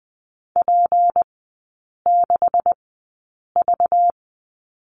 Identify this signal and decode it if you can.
Morse code, heard as a keyed tone, about 20 wpm, spaced as for 8 wpm, 700 Hz. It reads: P6V
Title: 191 Lesson 31e - Letter v - Three Characters 20wpm